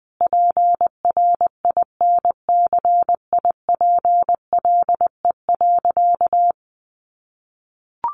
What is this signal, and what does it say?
Morse code, heard as a keyed tone, about 20 wpm, 700 Hz. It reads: PRINCIPLE.
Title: Sentences from Top 1400 Words 20wpm